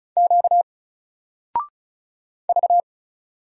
Morse code, heard as a keyed tone, about 36 wpm, 700 Hz. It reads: Q V
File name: Single-Letter-Number-35wpm